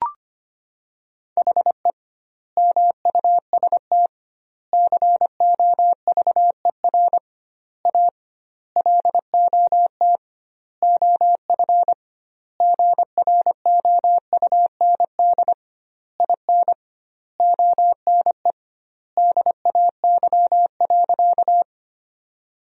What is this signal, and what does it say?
Morse code, heard as a keyed tone, about 25 wpm, 700 Hz. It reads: HE MUST COVER A LOT OF GROUND IN ONE DAY.
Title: 2x Word-Spacing - Sentences from Top 500 Words - 25wpm